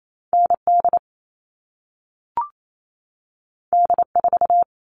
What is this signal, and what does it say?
Morse code, heard as a keyed tone, about 28 wpm, 700 Hz. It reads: ND D4
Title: All Cut Numbers - 2-Digit 28wpm